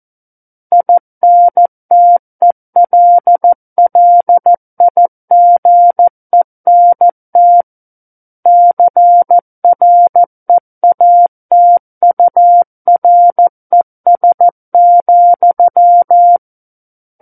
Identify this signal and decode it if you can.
Morse code, heard as a keyed tone, about 14 wpm, 700 Hz. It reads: INTELLIGENT CREATURES,
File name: Book One / 05-Chapter 5 - 14 WPM